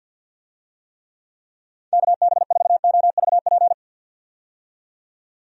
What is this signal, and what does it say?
Morse code, heard as a keyed tone, about 52 wpm, 700 Hz. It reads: KB4OVP
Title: USA Call Signs - 6 Characters - 50wpm